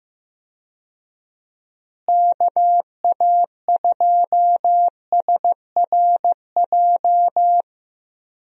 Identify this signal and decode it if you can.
Morse code, heard as a keyed tone, about 15 wpm, 700 Hz. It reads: KA2SRJ
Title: USA Call Signs - 6 Characters - Repeated 3x - 15wpm